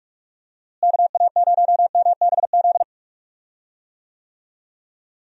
Morse code, heard as a keyed tone, about 46 wpm, 700 Hz. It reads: KA0MBZ
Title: USA Call Signs - 6 Characters - Repeated 2x - 45wpm